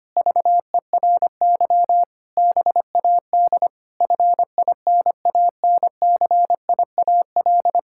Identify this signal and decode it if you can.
Morse code, heard as a keyed tone, about 25 wpm, 700 Hz. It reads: VERY BAD FINANCIAL
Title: Sentences from Top 1000 Words - Review Part 5 - 25wpm